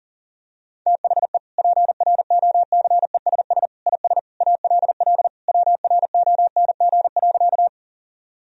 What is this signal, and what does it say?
Morse code, heard as a keyed tone, about 37 wpm, 700 Hz. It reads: THE PROCESS IS ALL WRONG.
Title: Sentences from Top 1000 Words - Review Part 3 - 40wpm